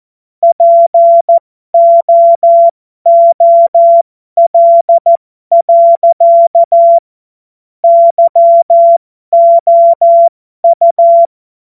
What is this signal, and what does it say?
Morse code, heard as a keyed tone, about 14 wpm, spaced as for 12 wpm, 680 Hz. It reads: POOL. YOU
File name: Morsekurs / Witze